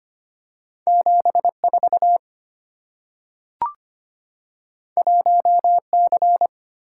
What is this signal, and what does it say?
Morse code, heard as a keyed tone, about 25 wpm, 700 Hz. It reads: 74 1C
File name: Two Letter-Number Combinations - 25wpm